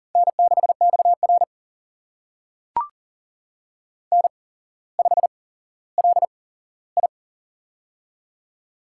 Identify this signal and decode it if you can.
Morse code, heard as a keyed tone, about 38 wpm, spaced as for 17 wpm, 700 Hz. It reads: N6XR N5LI
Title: USA Call Signs - 4 Characters - 8x Character Spacing - 40wpm